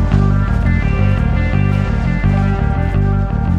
There is no code.